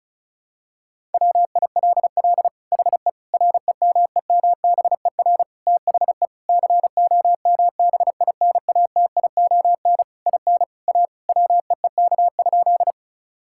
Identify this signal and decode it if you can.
Morse code, heard as a keyed tone, about 34 wpm, 700 Hz. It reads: WILL HE REMEMBER THE COMBINATION IN A WEEK?
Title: Sentences from Top 1000 Words - Review Part 6 - 35wpm